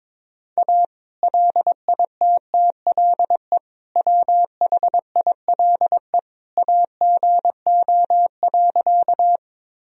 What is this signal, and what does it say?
Morse code, heard as a keyed tone, about 22 wpm, 700 Hz. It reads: A LITTLE WHILE AGO.
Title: Sentences from Top 1000 Words - Review Part 3 - 22wpm